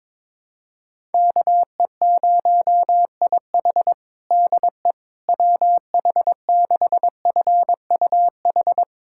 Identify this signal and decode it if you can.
Morse code, heard as a keyed tone, about 22 wpm, 700 Hz. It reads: KE0IH DE WH6FUH
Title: USA Callsign DE Callsign 22wpm